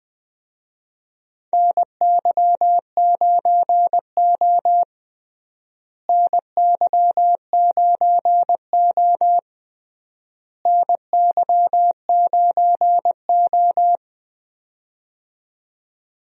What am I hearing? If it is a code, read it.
Morse code, heard as a keyed tone, about 21 wpm, 700 Hz. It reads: NY9O NY9O NY9O
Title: USA Call Signs - 4 Characters - Repeated 3x - 20wpm